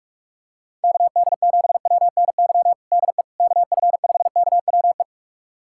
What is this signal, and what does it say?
Morse code, heard as a keyed tone, about 46 wpm, 700 Hz. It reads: KD7WNY DE KF5KWE
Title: USA Callsign DE Callsign - 6 Characters - 45wpm